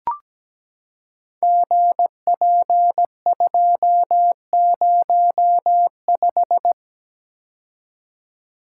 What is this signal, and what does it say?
Morse code, heard as a keyed tone, about 17 wpm, 700 Hz. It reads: GP205
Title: Code Groups - 5 Letter:Numbers - 17wpm